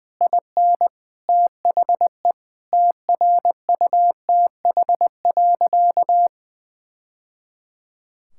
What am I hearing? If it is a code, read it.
Morse code, heard as a keyed tone, about 20 wpm, 700 Hz. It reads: IN THE TRUTH.
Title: Sentences from Top 1000 Words - Review Part 6 - 20wpm